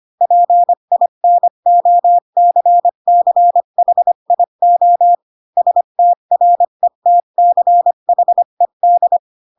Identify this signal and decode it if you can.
Morse code, heard as a keyed tone, about 25 wpm, 700 Hz. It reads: PINOCCHIO STRETCHED
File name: Pinocchio / Pinocchio ch7 20 25 ik1hge